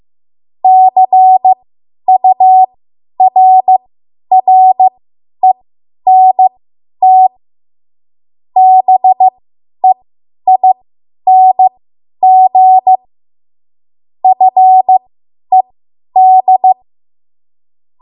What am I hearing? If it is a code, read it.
Morse code, heard as a keyed tone, about 15 wpm, spaced as for 10 wpm, 750 Hz. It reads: CURRENT BEING FED
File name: Archive / 10 WPM / 10WPM